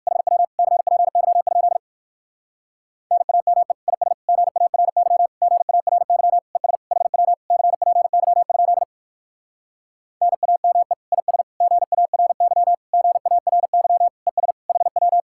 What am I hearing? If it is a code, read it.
Morse code, heard as a keyed tone, about 65 wpm, 700 Hz. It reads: HW CPY? NAME IS GARY GARY ES HW CPY? NAME IS GARY GARY ES HW